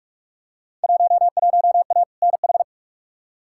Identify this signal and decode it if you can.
Morse code, heard as a keyed tone, about 46 wpm, 700 Hz. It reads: 11A NH
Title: Field Day Exchanges Encore - 45wpm